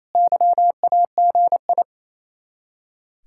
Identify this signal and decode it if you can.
Morse code, heard as a keyed tone, about 36 wpm, 700 Hz. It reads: YAGI